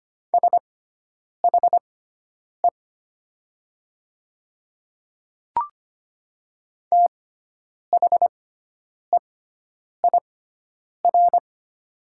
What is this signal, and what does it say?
Morse code, heard as a keyed tone, about 25 wpm, spaced as for 9 wpm, 700 Hz. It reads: SHE THEIR
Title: 6x Character Spacing - Top 100 Words - 25wpm